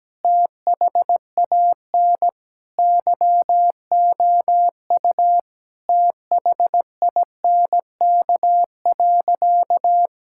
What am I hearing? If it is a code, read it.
Morse code, heard as a keyed tone, about 17 wpm, 700 Hz. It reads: THAN YOU THINK.
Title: Sentences from Top 1000 Words - Review Part 6 - 17wpm